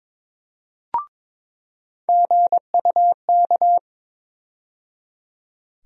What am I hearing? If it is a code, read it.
Morse code, heard as a keyed tone, about 22 wpm, 700 Hz. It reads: GUK
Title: USA Call Sign Postfixes - 3 Letters 22wpm